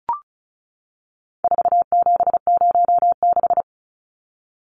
Morse code, heard as a keyed tone, about 34 wpm, 700 Hz. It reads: 4706
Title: Numbers - 4-Digit 35wpm